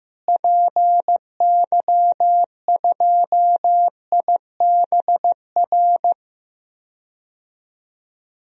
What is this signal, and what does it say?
Morse code, heard as a keyed tone, about 15 wpm, 700 Hz. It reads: PY2IBR
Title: DXCC-100 Call Signs - 6 Characters - Repeated 2x - 15wpm